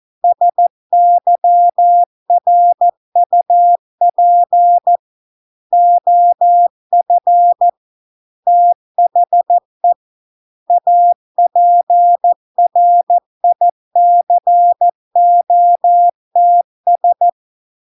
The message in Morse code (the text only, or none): SYRUP OF THE APRICOTS